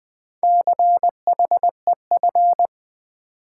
Morse code, heard as a keyed tone, about 20 wpm, 700 Hz. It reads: CHEF
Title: Build A Word - 2 Letters At A Time - 4-Letter Words - 20wpm